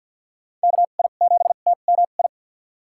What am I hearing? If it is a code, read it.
Morse code, heard as a keyed tone, about 50 wpm, spaced as for 36 wpm, 700 Hz. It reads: KI7TMI
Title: USA Call Signs - 2x Character Spacing - 50wpm